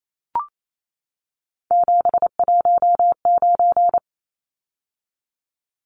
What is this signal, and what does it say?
Morse code, heard as a keyed tone, about 28 wpm, 700 Hz. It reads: 719
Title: Numbers - 3-Digit 28wpm